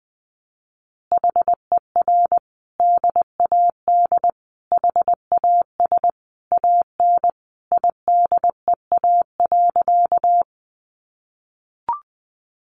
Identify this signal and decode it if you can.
Morse code, heard as a keyed tone, about 20 wpm, 700 Hz. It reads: HER DAD HAS AN IDEA.
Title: Easy Phrases 20wpm